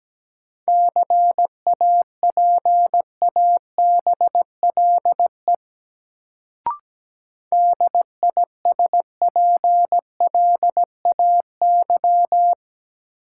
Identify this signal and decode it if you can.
Morse code, heard as a keyed tone, about 17 wpm, 700 Hz. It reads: CAPABLE DISPLAY
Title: Sentences from Top 1200 Words - New Word Drill Alt - 17wpm